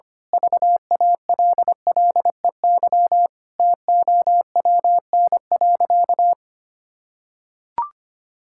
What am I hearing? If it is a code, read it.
Morse code, heard as a keyed tone, about 25 wpm, 700 Hz. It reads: VALLEY TOWN.
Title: Pangram Challenge v2 - 25wpm